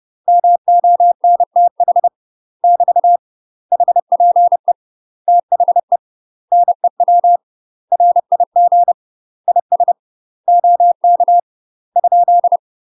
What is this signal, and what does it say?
Morse code, heard as a keyed tone, about 30 wpm, 700 Hz. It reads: MONTH <BT> HPE THE NEW RIG IS OK ?
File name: UK MASTER QSOs TRAINING / 30WPM / UK MASTER QSOs 0028